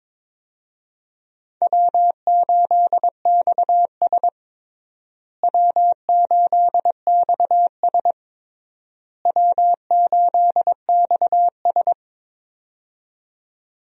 Morse code, heard as a keyed tone, about 22 wpm, 700 Hz. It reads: W8XS W8XS W8XS
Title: USA Call Signs - 4 Characters - Repeated 3x - 22wpm